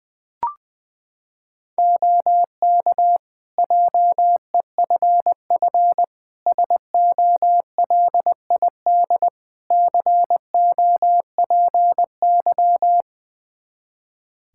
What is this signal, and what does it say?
Morse code, heard as a keyed tone, about 20 wpm, 700 Hz. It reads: OK JEFF SOLID COPY
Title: K1USN SST QSO Practice 20wpm